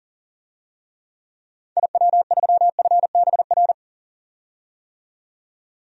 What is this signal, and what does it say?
Morse code, heard as a keyed tone, about 40 wpm, 700 Hz. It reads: IW3FBR
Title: DXCC-100 Call Signs - 6 Characters - 40wpm